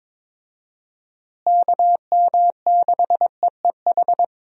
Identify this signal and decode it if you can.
Morse code, heard as a keyed tone, about 22 wpm, 700 Hz. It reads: KM6EEH
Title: USA Call Signs - 6 Characters - Repeated 2x - 22wpm